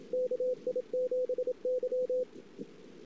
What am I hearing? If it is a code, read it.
Morse code, heard as a keyed tone, about 27 wpm, 500 Hz, with radio noise. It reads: KI7Y
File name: CW / 2008-ss-cw-k5tr-op-n5rz -19:00